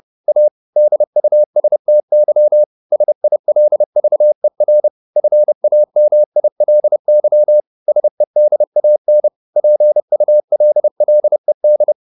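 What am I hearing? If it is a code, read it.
Morse code, heard as a keyed tone, about 30 wpm, 600 Hz. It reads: A DUSTY SILVER FAMILY SEDAN PULLED
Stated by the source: Morse code at 30 wpm, the novel Never by Ken Follett in English, from CW med Gnister.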